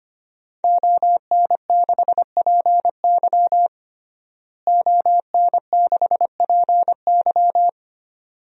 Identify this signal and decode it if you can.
Morse code, heard as a keyed tone, about 25 wpm, 700 Hz. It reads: ON6PY ON6PY
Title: DXCC-100 Call Signs - 5 Characters - Repeated 2x - 25wpm